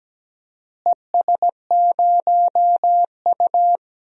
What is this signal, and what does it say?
Morse code, heard as a keyed tone, about 17 wpm, 700 Hz. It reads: ES0U
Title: DXCC-100 Call Signs - 4 Characters - Repeated 3x - 17wpm